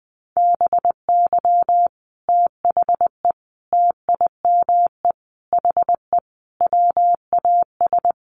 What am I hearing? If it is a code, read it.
Morse code, heard as a keyed tone, about 20 wpm, 700 Hz. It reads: BY THE TIME HE WAS
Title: Sets of 5 Words Encore 20wpm